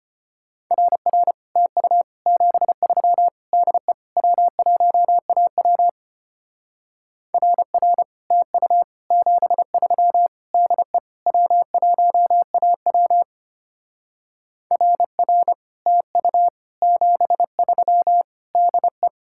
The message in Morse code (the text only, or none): RR TU 73 DE W1AW RR TU 73 DE W1AW RR TU 73 DE